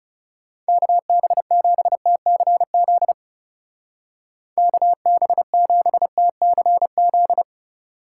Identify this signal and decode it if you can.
Morse code, heard as a keyed tone, about 34 wpm, 700 Hz. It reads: KB7TCZ KB7TCZ
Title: Speed Racing - USA 2x3 Call Signs 30wpm